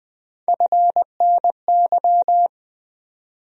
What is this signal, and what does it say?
Morse code, heard as a keyed tone, about 20 wpm, 700 Hz. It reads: FNY
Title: USA Call Sign Postfixes - 3 Letters 20wpm